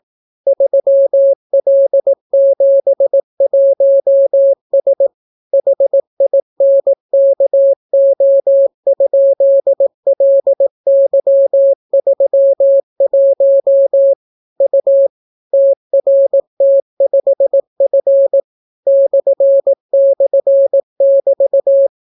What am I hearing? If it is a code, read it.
Morse code, heard as a keyed tone, about 18 wpm, 550 Hz. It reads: 3L71S HINKO?LY31 U TRT5F //<BT>